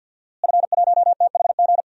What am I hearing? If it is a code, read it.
Morse code, heard as a keyed tone, about 47 wpm, 700 Hz. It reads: F1THG
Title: DXCC-100 Call Signs - 5 Characters - Repeated 3x - 50wpm